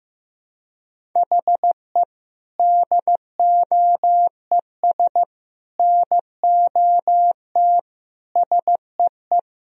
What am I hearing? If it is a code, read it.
Morse code, heard as a keyed tone, about 16 wpm, 700 Hz. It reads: HE DOES NOT SEE
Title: Sentences from Top 1000 Words - Review Part 1 - 15wpm